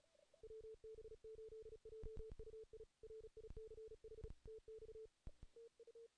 Morse code, heard as a keyed tone, about 37 wpm, 440 Hz. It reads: WB8JUI RICH TX TU
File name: 2016 JAN CW NAQP / 2016 Jan naqp k5ot - 0-2